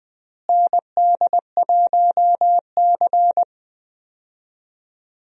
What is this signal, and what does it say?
Morse code, heard as a keyed tone, about 20 wpm, 700 Hz. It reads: ND1C
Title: USA Call Signs - 4 Characters - Repeated 3x - 20wpm